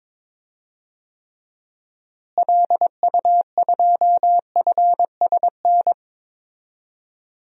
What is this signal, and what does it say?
Morse code, heard as a keyed tone, about 22 wpm, 700 Hz. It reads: LU2FSN
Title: DXCC-100 Call Signs - 6 Characters - Repeated 3x - 22wpm